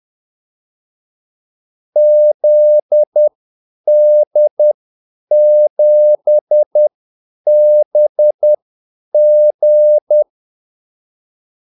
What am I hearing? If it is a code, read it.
Morse code, heard as a keyed tone, about 10 wpm, spaced as for 8 wpm, 600 Hz. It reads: ZD7BG
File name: Callsigns-10-8wpm